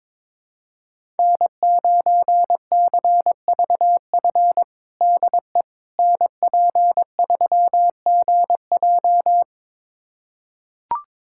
Morse code, heard as a keyed tone, about 22 wpm, 700 Hz. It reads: N9CVF DE NP3GJ
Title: USA Callsign DE Callsign - 5 Characters - 22wpm